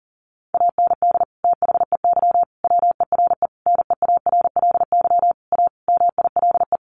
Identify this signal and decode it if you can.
Morse code, heard as a keyed tone, about 40 wpm, 700 Hz. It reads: AND THEY WERE NEARLY A MILE
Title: Sentences from Top 1300 Words 40wpm